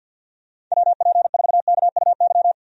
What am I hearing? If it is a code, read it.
Morse code, heard as a keyed tone, about 50 wpm, 700 Hz. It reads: WP4KUY
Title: USA Call Signs - 6 Characters - 50wpm